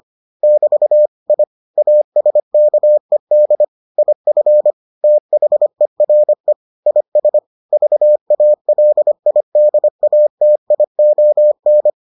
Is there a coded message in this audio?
Yes: Morse code from 0.4 to 11.9 s, heard as a keyed tone, about 25 wpm, 600 Hz. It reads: <BT> I ASKED IF THERE IS VALIDATION